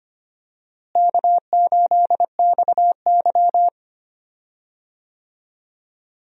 Morse code, heard as a keyed tone, about 25 wpm, 700 Hz. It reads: K8XY
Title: USA Call Signs - 4 Characters - Repeated 3x - 25wpm